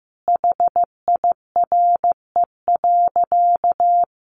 Morse code, heard as a keyed tone, about 15 wpm, 700 Hz. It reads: HIRE.
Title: Sentences from Top 1400 Words 15wpm